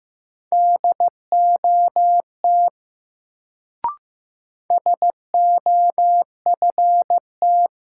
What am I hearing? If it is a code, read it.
Morse code, heard as a keyed tone, about 15 wpm, 700 Hz. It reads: DOT SOFT
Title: Sentences from Top 1500 Words - New Word Drill - 15wpm